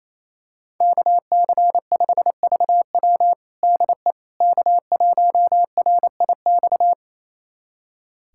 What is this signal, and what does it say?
Morse code, heard as a keyed tone, about 28 wpm, 700 Hz. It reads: KC5VW DE K1RIX
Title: USA Callsign DE Callsign 28wpm